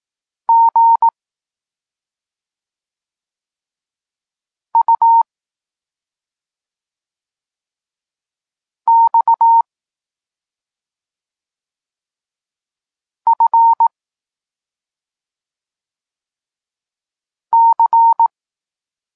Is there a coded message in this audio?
Yes: Morse code from 0.5 to 18.3 s, heard as a keyed tone, about 18 wpm, 920 Hz. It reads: G U X F C